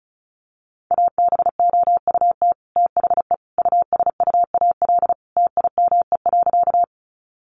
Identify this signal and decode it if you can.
Morse code, heard as a keyed tone, about 34 wpm, 700 Hz. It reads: ABOUT THE USUAL TIME.
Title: Sentences from Top 400 Words 35wpm